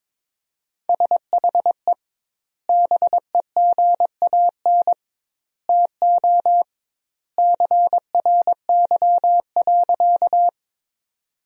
Morse code, heard as a keyed tone, about 22 wpm, 700 Hz. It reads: SHE BEGAN TO CRY.
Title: 2x Word-Spacing - Sentences from Top 500 Words - 22wpm